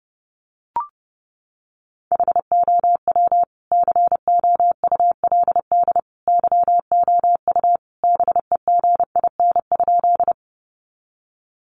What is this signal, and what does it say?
Morse code, heard as a keyed tone, about 30 wpm, 700 Hz. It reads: HOW COULD YOU BEGIN?
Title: Sentences from Top 300 Words 30wpm